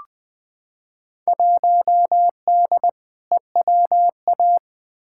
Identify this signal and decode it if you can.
Morse code, heard as a keyed tone, about 20 wpm, 700 Hz. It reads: 1D EWA
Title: Field Day Exchanges - 20wpm